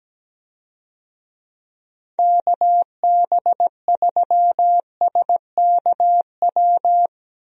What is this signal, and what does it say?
Morse code, heard as a keyed tone, about 17 wpm, 700 Hz. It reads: KB3SKW
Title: USA Call Signs - 6 Characters - Repeated 3x - 17wpm